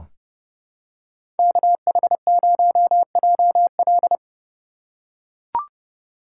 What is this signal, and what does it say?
Morse code, heard as a keyed tone, about 30 wpm, 700 Hz. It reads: KH0JL
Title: USA Call Signs - 5 Characters - 2x Character Spacing - 30wpm